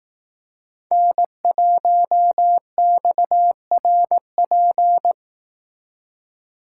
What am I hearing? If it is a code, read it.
Morse code, heard as a keyed tone, about 18 wpm, 700 Hz. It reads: N1XRP
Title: Speed Racing - USA 1x3 Call Signs 15wpm